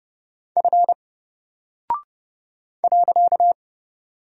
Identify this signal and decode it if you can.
Morse code, heard as a keyed tone, about 31 wpm, 700 Hz. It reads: F .
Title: Single Character (Letter:Number:Punctuation:Prosign) 30wpm